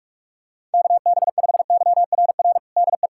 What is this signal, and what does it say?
Morse code, heard as a keyed tone, about 44 wpm, 700 Hz. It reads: KB5YRR DE
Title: USA Callsign DE Callsign 45wpm